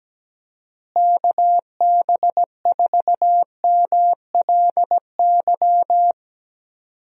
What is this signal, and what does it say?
Morse code, heard as a keyed tone, about 17 wpm, 700 Hz. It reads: KB4MLY
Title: USA Call Signs - 6 Characters - Repeated 2x - 17wpm